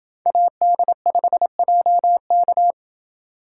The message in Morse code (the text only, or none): AD5JK